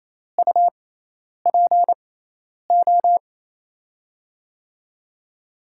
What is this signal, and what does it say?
Morse code, heard as a keyed tone, about 28 wpm, spaced as for 10 wpm, 700 Hz. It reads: UPO
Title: USA Call Signs - 6 Characters - 6x Character Spacing - 28wpm